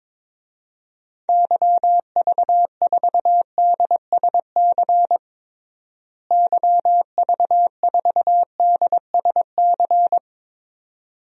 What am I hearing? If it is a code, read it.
Morse code, heard as a keyed tone, about 22 wpm, 700 Hz. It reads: YV4DSC YV4DSC
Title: DXCC-100 Call Signs - 6 Characters - Repeated 2x - 22wpm